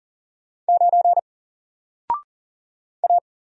Morse code, heard as a keyed tone, about 40 wpm, 700 Hz. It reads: 9 A
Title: Single Letter-Number 40wpm